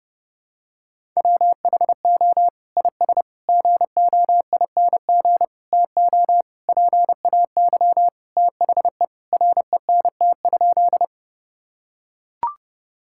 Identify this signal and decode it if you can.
Morse code, heard as a keyed tone, about 30 wpm, 700 Hz. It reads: WHO IS GOING TO PAY THE RENT?
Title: Sentences from Top 1000 Words - Review Part 5 - 30wpm